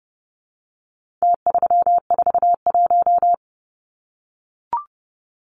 Morse code, heard as a keyed tone, about 30 wpm, 700 Hz. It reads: T341
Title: Numbers - Contest Style - 4-Digit 30wpm